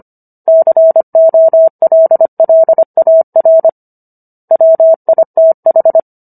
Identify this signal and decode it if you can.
Morse code, heard as a keyed tone, about 25 wpm, 650 Hz. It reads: COLLAR WITH